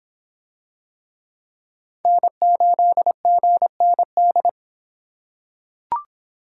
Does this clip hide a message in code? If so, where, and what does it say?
Yes: Morse code from 2.1 to 4.5 s, heard as a keyed tone, about 23 wpm, 700 Hz. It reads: N8GND